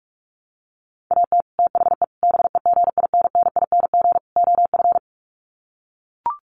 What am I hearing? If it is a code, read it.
Morse code, heard as a keyed tone, about 44 wpm, 700 Hz. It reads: AT THE BEGINNING OF
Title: Sets of 4 Words 45wpm